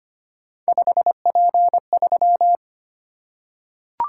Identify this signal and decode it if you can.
Morse code, heard as a keyed tone, about 25 wpm, 700 Hz. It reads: 5P3
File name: Three Letter-Number Combinations - 25wpm